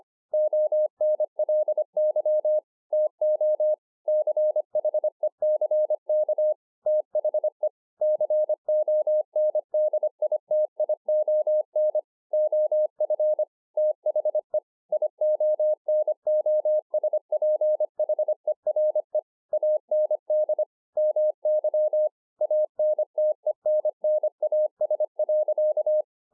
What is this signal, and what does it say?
Morse code, heard as a keyed tone, about 25 wpm, 610 Hz. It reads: ONLY TO CHECK THE CONDITION OF THE IONOSPHERE AND MY ANTENNAS.